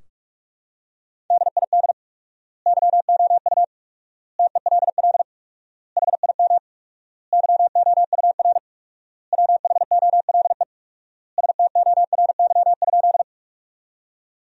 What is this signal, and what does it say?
Morse code, heard as a keyed tone, about 46 wpm, 700 Hz. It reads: DID YOU TELL HIM YOUR WHOLE STORY?
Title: 4x Word-Spacing - Sentences from Top 400 Words - 45wpm